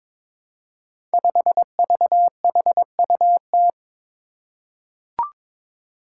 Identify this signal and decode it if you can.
Morse code, heard as a keyed tone, about 22 wpm, 700 Hz. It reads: 5VHUT
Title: Five Challenging Letter-Number Combinations 22wpm